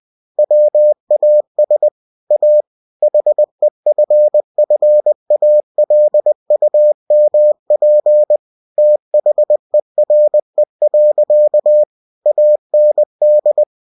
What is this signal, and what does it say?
Morse code, heard as a keyed tone, about 20 wpm, 600 Hz. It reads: WAS A HEFFALUMP THERE. AND
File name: cw-books / winnie-the-pooh / winnie-20wpm-page-0041